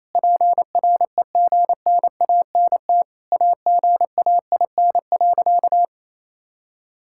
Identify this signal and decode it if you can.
Morse code, heard as a keyed tone, about 28 wpm, 700 Hz. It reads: PREGNANT AGAIN.
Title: Sentences from Top 1000 Words - Review Part 6 - 28wpm